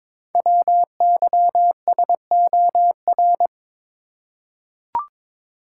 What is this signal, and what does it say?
Morse code, heard as a keyed tone, about 22 wpm, 700 Hz. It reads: WYSOR
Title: Code Groups - 5 Letters - 22wpm